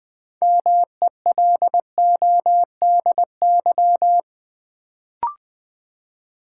Morse code, heard as a keyed tone, about 20 wpm, 700 Hz. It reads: MELODY
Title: Build A Word - 2 Letters At A Time - 6-Letter Words - 20wpm